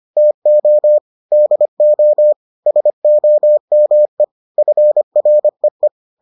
Morse code, heard as a keyed tone, about 25 wpm, 600 Hz. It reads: TO DO SOME FREE